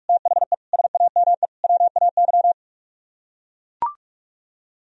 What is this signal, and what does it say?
Morse code, heard as a keyed tone, about 44 wpm, 700 Hz. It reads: THE SAME WAY